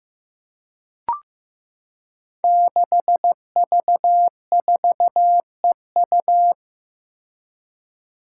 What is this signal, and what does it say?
Morse code, heard as a keyed tone, about 15 wpm, 700 Hz. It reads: 6V4EU
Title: Five Challenging Letter-Number Combinations 15wpm